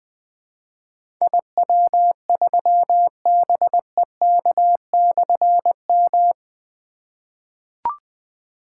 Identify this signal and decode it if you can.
Morse code, heard as a keyed tone, about 20 wpm, 700 Hz. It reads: IW3BEK/M
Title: Difficult Call Signs 20wpm